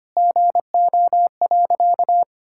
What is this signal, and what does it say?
Morse code, heard as a keyed tone, about 25 wpm, 700 Hz. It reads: GO.
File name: practice / 25wpm / Sentences with 4-Word Phrases - 25wpm